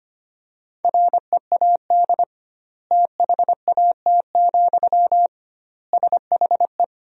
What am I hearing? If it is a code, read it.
Morse code, heard as a keyed tone, about 25 wpm, 700 Hz. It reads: READ THAT, SHE
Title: 2x Word-Spacing - Sentences from Top 200 Words - 25wpm